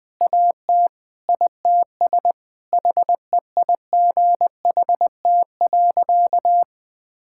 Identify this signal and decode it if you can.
Morse code, heard as a keyed tone, about 20 wpm, 700 Hz. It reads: AT ITS HEIGHT.
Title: Sentences from Top 1600 Words 20wpm